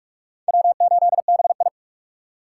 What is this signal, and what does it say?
Morse code, heard as a keyed tone, about 44 wpm, 700 Hz. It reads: W8BI